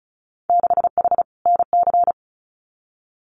Morse code, heard as a keyed tone, about 36 wpm, 700 Hz. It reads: 6H NC
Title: Winter Field Day Exchanges Encore 35wpm